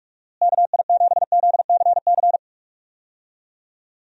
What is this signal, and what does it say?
Morse code, heard as a keyed tone, about 42 wpm, 700 Hz. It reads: KI7ZCC
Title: USA Call Signs - 6 Characters - Repeated 2x - 45wpm